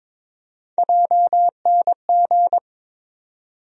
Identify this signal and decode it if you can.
Morse code, heard as a keyed tone, about 22 wpm, 700 Hz. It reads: JNG
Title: USA Call Sign Postfixes - 3 Letters 22wpm